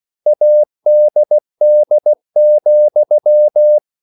Morse code, heard as a keyed tone, about 16 wpm, 600 Hz. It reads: ADD,
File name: CW / Practice / EnglishQuotations0216